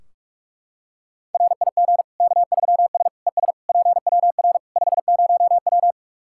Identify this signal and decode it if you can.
Morse code, heard as a keyed tone, about 41 wpm, 700 Hz. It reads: RIG K3S ES PWR 50W
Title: K1USN SST QSO Practice 45wpm